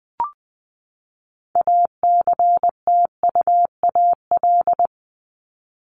Morse code, heard as a keyed tone, about 20 wpm, 700 Hz. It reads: ACTUAL
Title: LY Suffix - Top 100 Words with Base Words 20wpm